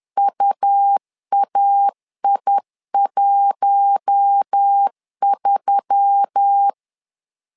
Morse code, heard as a keyed tone, about 10 wpm, 800 Hz. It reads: UAI13
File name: LEZIONI / 5 LBVC / 5 L 10wpm 2259